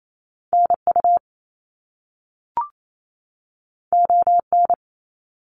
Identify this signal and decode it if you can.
Morse code, heard as a keyed tone, about 28 wpm, 700 Hz. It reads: NU ON
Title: Canadian Provinces and Territories 28wpm